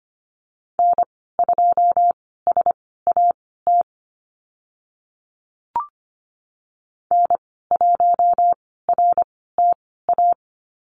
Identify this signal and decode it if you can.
Morse code, heard as a keyed tone, about 25 wpm, spaced as for 16 wpm, 700 Hz. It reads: N2SAT N1RTA
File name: Lesson 15g - Number 5 - Call Signs 25wpm